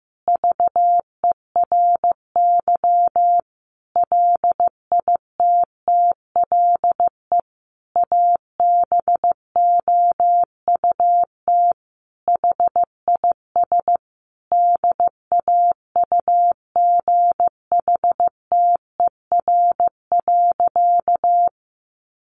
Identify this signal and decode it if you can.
Morse code, heard as a keyed tone, about 15 wpm, 700 Hz. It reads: VERY LITTLE ABOUT HIS DAUGHTER.
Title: Sentences from Top 1500 Words 15wpm